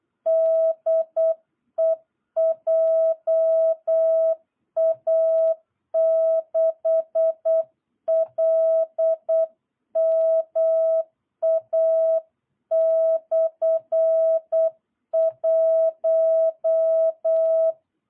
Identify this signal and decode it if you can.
Morse code, heard as a keyed tone, about 8 wpm, 650 Hz. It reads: DE JA6LMA/1
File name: jh8cbh / 2022cw